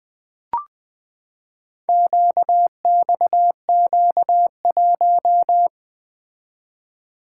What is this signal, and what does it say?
Morse code, heard as a keyed tone, about 20 wpm, 700 Hz. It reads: QXQ1
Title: Four Letter-Number Combinations - 20wpm